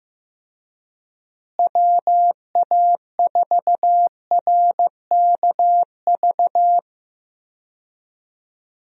Morse code, heard as a keyed tone, about 15 wpm, 700 Hz. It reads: WA4RKV
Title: USA Call Signs - 6 Characters - Repeated 3x - 15wpm